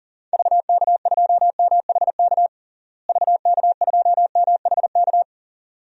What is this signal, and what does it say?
Morse code, heard as a keyed tone, about 40 wpm, 700 Hz. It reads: VK2MHK VK2MHK
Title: DXCC-100 Call Signs - 6 Characters - Repeated 2x - 40wpm